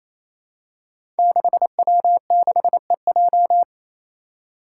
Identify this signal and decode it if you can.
Morse code, heard as a keyed tone, about 28 wpm, 700 Hz. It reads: 6W6EJ
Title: Five Challenging Letter-Number Combinations 28wpm